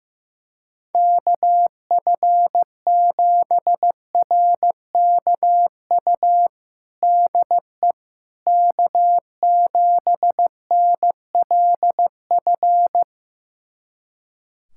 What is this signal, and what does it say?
Morse code, heard as a keyed tone, about 15 wpm, 700 Hz. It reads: KF7RKU DE K7NLF
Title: USA Callsign DE Callsign 15wpm